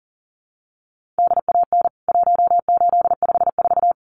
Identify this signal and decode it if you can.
Morse code, heard as a keyed tone, about 40 wpm, 700 Hz. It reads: DAN 1854
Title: CWT Exchange 40wpm